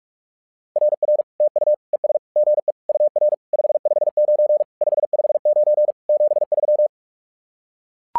Morse code, heard as a keyed tone, about 44 wpm, 600 Hz. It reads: RR TU ES GE UR 559 559 73
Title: SOTA QSO Practice 45wpm